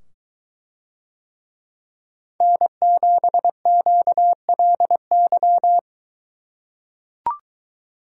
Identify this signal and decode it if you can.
Morse code, heard as a keyed tone, about 22 wpm, 700 Hz. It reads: N7QLY